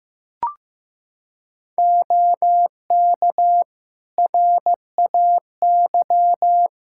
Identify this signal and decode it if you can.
Morse code, heard as a keyed tone, about 15 wpm, 700 Hz. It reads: OK RAY
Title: Practice QSO - Exchange 2 - Info - 15wpm